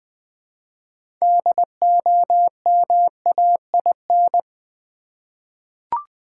Morse code, heard as a keyed tone, about 20 wpm, 700 Hz. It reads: DOMAIN
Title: 6 Letter Words - First 3 Letters Given - 20wpm